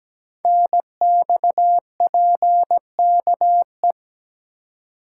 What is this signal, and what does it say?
Morse code, heard as a keyed tone, about 17 wpm, 700 Hz. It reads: NXPKE
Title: Code Groups - 5 Letters - 17wpm